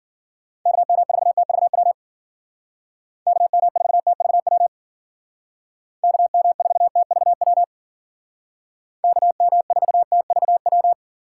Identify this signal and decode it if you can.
Morse code, heard as a keyed tone, about 52 wpm, 700 Hz. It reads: KM4TVW KM4TVW KM4TVW KM4TVW